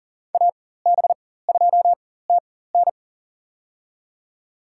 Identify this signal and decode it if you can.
Morse code, heard as a keyed tone, about 40 wpm, spaced as for 19 wpm, 700 Hz. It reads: AB2TN
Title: USA Call Signs - 5 Characters - 4x Character Spacing - 40wpm